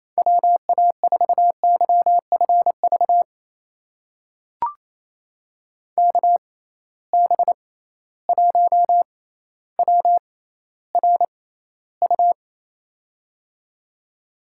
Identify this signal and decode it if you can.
Morse code, heard as a keyed tone, about 28 wpm, spaced as for 14 wpm, 700 Hz. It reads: WA4YFV KB1WRU
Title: USA Call Signs - 6 Characters - 6x Character Spacing - 28wpm